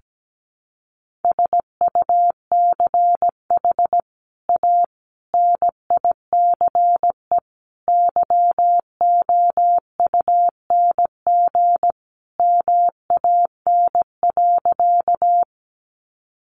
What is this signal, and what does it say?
Morse code, heard as a keyed tone, about 17 wpm, 700 Hz. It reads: SUCH A NICE YOUNG MAN.
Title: Sentences from Top 1100 Words 17wpm